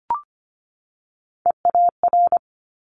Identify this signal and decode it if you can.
Morse code, heard as a keyed tone, about 25 wpm, 700 Hz. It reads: EAR
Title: Multi-letter Phonemes 25wpm